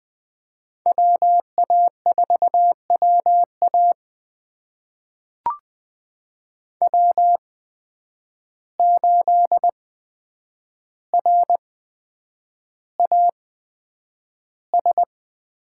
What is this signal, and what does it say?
Morse code, heard as a keyed tone, about 20 wpm, spaced as for 9 wpm, 700 Hz. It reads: WA4WA W8RAS
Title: USA Call Signs - 5 Characters - 8x Character Spacing - 20wpm